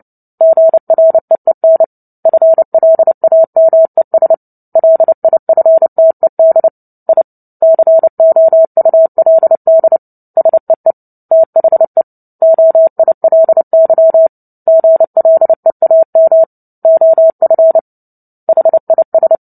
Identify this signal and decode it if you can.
Morse code, heard as a keyed tone, about 30 wpm, 650 Hz. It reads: GREEN FLAMES LIFTED I COULD SEE THE OILY GLEAM OF HIS
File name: Book Two / 03-Chapter 3 - 29 WPM